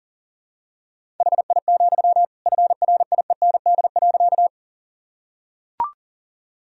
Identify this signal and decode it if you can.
Morse code, heard as a keyed tone, about 37 wpm, 700 Hz. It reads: HI, FRIEND.
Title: Sentences from Top 1000 Words - Review Part 3 - 40wpm